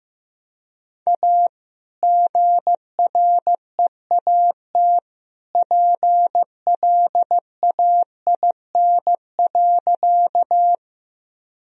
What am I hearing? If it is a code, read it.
Morse code, heard as a keyed tone, about 15 wpm, 700 Hz. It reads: A GREAT PLAIN.
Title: Sentences from Top 1000 Words - Review Part 2 - 15wpm